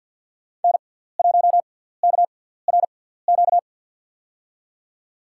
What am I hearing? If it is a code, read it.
Morse code, heard as a keyed tone, about 48 wpm, spaced as for 17 wpm, 700 Hz. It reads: N1KRQ
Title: USA Call Signs - 5 Characters - 6x Character Spacing - 50wpm